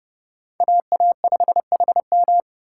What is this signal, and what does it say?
Morse code, heard as a keyed tone, about 29 wpm, 700 Hz. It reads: AA5HM